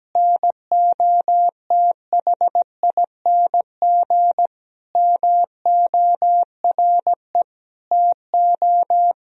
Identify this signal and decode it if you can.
Morse code, heard as a keyed tone, about 17 wpm, 700 Hz. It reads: NOTHING MORE TO
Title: Sentences from Top 1000 Words - Review Part 4 - 17wpm